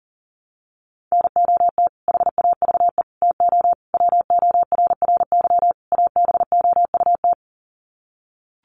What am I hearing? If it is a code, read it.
Morse code, heard as a keyed tone, about 40 wpm, 700 Hz. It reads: NOT HAVE TO WORRY ABOUT
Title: Sets of 5 Words 40wpm